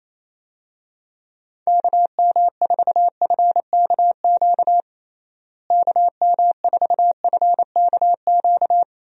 Morse code, heard as a keyed tone, about 28 wpm, 700 Hz. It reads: KM4FKQ KM4FKQ
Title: USA Call Signs - 6 Characters - Repeated 2x - 28wpm